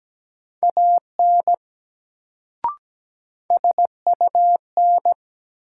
Morse code, heard as a keyed tone, about 17 wpm, 700 Hz. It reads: AN SUN
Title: Sentences from Top 300 Words - New Word Drill - 17wpm